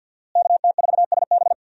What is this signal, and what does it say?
Morse code, heard as a keyed tone, about 51 wpm, 700 Hz. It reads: KT4SB
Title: USA Call Signs - 5 Characters - Repeated 3x - 50wpm